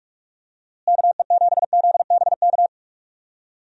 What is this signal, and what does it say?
Morse code, heard as a keyed tone, about 46 wpm, 700 Hz. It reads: KE7ZBK
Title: USA Call Signs - 6 Characters - Repeated 3x - 45wpm